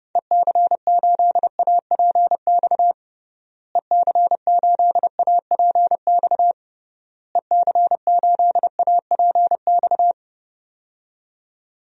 Morse code, heard as a keyed tone, about 28 wpm, 700 Hz. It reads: EC8APX EC8APX EC8APX
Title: DXCC-100 Call Signs - 6 Characters - Repeated 3x - 30wpm